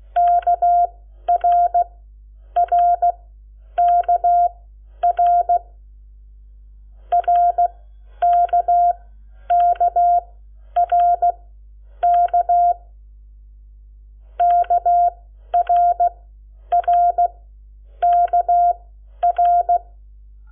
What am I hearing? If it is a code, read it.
Morse code, heard as a keyed tone, about 16 wpm, spaced as for 10 wpm, 700 Hz. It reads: KRRKR RKKRK KRRKR
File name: Lecciones / CLASE Nº7